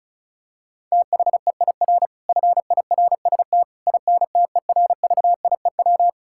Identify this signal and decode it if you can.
Morse code, heard as a keyed tone, about 34 wpm, 700 Hz. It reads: THEIR FIRST INTERVIEW
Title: Sentences from Top 1600 Words 35wpm